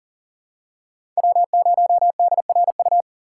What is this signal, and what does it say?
Morse code, heard as a keyed tone, about 41 wpm, 700 Hz. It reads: W0DRU
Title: USA Call Signs - 5 Characters - 6x Character Spacing - 40wpm